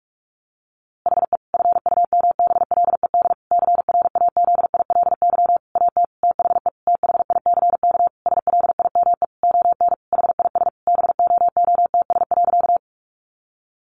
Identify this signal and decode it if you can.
Morse code, heard as a keyed tone, about 44 wpm, 700 Hz. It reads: HE FUMBLED CRAZILY AT THE THICK SLIME ON HIS BOOTS.
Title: A Sound Of Thunder 45wpm